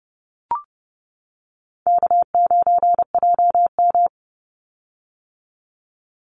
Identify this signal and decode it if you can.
Morse code, heard as a keyed tone, about 30 wpm, 700 Hz. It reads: K9JM
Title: Easy Call Signs - 30wpm